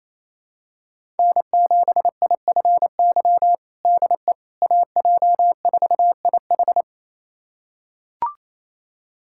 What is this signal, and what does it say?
Morse code, heard as a keyed tone, about 28 wpm, 700 Hz. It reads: N7IFY DE AJ4IH
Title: USA Callsign DE Callsign - 5 Characters - 28wpm